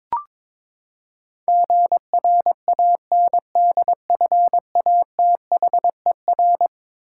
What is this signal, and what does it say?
Morse code, heard as a keyed tone, about 22 wpm, 700 Hz. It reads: GRANDFATHER
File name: practice / 22wpm / Eleven Letter Words - 22wpm